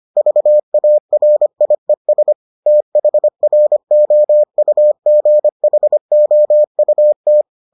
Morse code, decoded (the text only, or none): VARIES THROUGHOUT